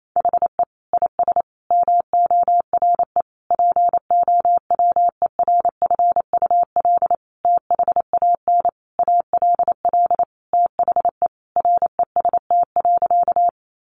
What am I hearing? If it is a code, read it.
Morse code, heard as a keyed tone, about 28 wpm, 700 Hz. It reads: HE IS MORE POWERFUL THAN ALL THE REST.
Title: Sentences from Top 1300 Words 28wpm